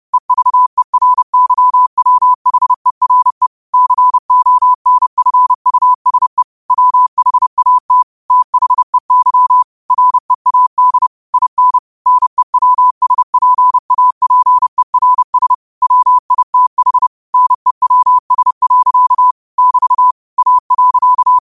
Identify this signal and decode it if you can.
Morse code, heard as a keyed tone, about 30 wpm, 1000 Hz. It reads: EVERYWHERE CONFUSE WHAT THEY READ IN NEWSPAPERS WITH NEWS. <BT> A.
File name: qotdfiles / 30wpm